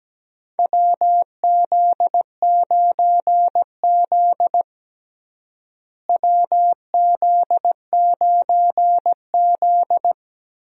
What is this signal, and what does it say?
Morse code, heard as a keyed tone, about 17 wpm, 700 Hz. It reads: WZ9Z WZ9Z
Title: USA Call Signs - 4 Characters - Repeated 3x - 17wpm